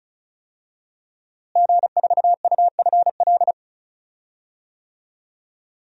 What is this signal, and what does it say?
Morse code, heard as a keyed tone, about 34 wpm, 700 Hz. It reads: G4UFL
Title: DXCC-100 Call Signs - 5 Characters - Repeated 3x - 35wpm